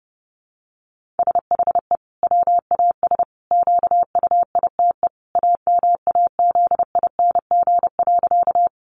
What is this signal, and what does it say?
Morse code, heard as a keyed tone, about 30 wpm, 700 Hz. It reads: SHE WAS QUITE AMAZING.
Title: Sentences from Top 1500 Words 30wpm